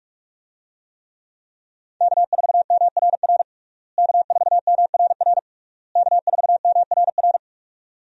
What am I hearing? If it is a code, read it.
Morse code, heard as a keyed tone, about 41 wpm, 700 Hz. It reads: K4MRR K4MRR K4MRR
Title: USA Call Signs - 5 Characters - Repeated 3x - 45wpm